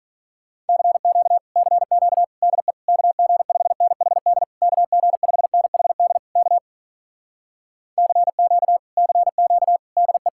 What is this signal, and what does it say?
Morse code, heard as a keyed tone, about 47 wpm, 700 Hz. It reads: CQ CQ DE KG5NHD KG5NHD K CQ CQ DE